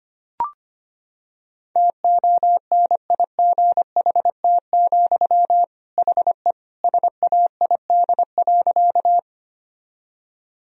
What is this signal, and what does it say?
Morse code, heard as a keyed tone, about 25 wpm, 700 Hz. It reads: TONIGHT, HE SAID.
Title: Sentences from Top 1000 Words - Review Part 3 - 25wpm